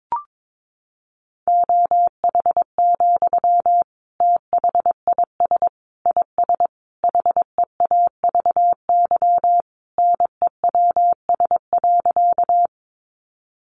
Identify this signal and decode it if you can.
Morse code, heard as a keyed tone, about 22 wpm, 700 Hz. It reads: OH, THIS IS HEAVY NEWS.
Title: Sentences from Top 1100 Words 22wpm